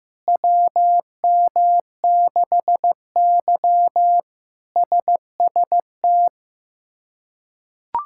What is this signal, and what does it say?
Morse code, heard as a keyed tone, about 15 wpm, 700 Hz. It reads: WM6Y SST
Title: K1USN SST QSO Practice - 15wpm